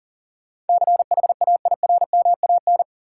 Morse code, heard as a keyed tone, about 40 wpm, 700 Hz. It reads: CHAIRMAN
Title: Sets of 3 Words - New Word Drill - 40wpm